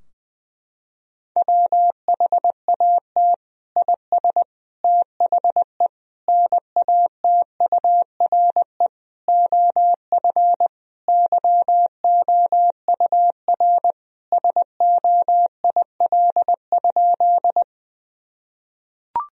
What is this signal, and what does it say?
Morse code, heard as a keyed tone, about 20 wpm, 700 Hz. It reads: WHAT IS THE NATURE OF YOUR SOIL?
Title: Sentences from Top 1000 Words - Review Part 4 - 20wpm